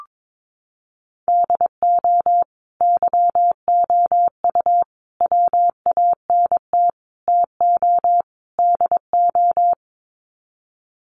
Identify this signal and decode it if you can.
Morse code, heard as a keyed tone, about 22 wpm, 700 Hz. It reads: DO YOU WANT TO DO
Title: Sets of 5 Words Encore 22wpm